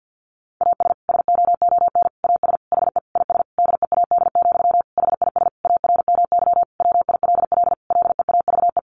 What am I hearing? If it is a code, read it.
Morse code, heard as a keyed tone, about 50 wpm, 700 Hz. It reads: AS SOON AS HE IS DEAD, HIS ARMY WILL LEAVE
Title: Sentences from Top 1500 Words 50wpm